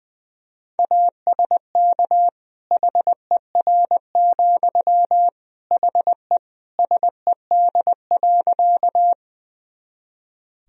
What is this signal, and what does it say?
Morse code, heard as a keyed tone, about 21 wpm, 700 Hz. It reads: ASK HER, HE SED.
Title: Sentences from Top 200 Words and Abbreviations 20wpm